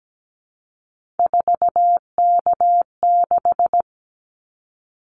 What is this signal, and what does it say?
Morse code, heard as a keyed tone, about 17 wpm, 700 Hz. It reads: 4K6
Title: Callsign Prefixes for Top 100 DXCC - 17wpm